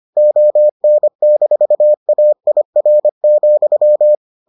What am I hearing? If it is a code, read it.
Morse code, heard as a keyed tone, about 25 wpm, 600 Hz. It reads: ON-AIR,